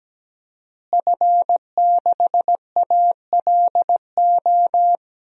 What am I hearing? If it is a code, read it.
Morse code, heard as a keyed tone, about 17 wpm, 700 Hz. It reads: F6ALO
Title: DXCC-100 Call Signs - 5 Characters - Repeated 3x - 17wpm